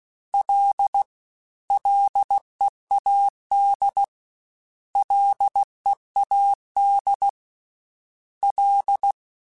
Morse code, heard as a keyed tone, about 16 wpm, 800 Hz. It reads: L LEAD LEAD L
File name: Ls / Single L m3s / Single L Prefixe 3x2 Twice @15